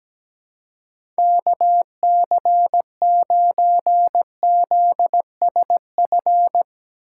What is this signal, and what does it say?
Morse code, heard as a keyed tone, about 17 wpm, 700 Hz. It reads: KC9ZSF
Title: USA Call Signs - 6 Characters - 4x Character Spacing - 17wpm